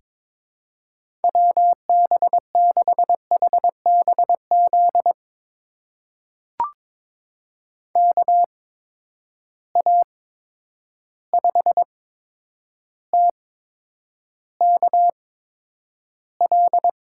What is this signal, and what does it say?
Morse code, heard as a keyed tone, about 22 wpm, spaced as for 9 wpm, 700 Hz. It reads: WB6HBZ KA5TKL
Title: USA Call Signs - 6 Characters - 8x Character Spacing - 22wpm